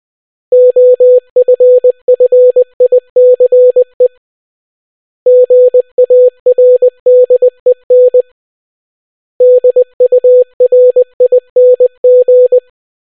Morse code, heard as a keyed tone, about 20 wpm, 500 Hz. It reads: OFFICE GARDEN DURING